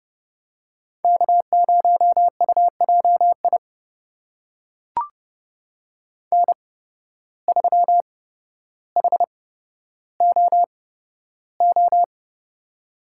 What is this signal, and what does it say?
Morse code, heard as a keyed tone, about 30 wpm, spaced as for 13 wpm, 700 Hz. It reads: K0UJI N3HOO
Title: USA Call Signs - 5 Characters - 8x Character Spacing - 30wpm